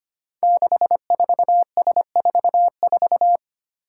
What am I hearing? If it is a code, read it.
Morse code, heard as a keyed tone, about 25 wpm, 700 Hz. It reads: 64S44
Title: Five Challenging Letter-Number Combinations 25wpm